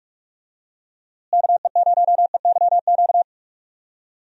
Morse code, heard as a keyed tone, about 46 wpm, 700 Hz. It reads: KE0EYQ
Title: USA Call Signs - 6 Characters - Repeated 2x - 45wpm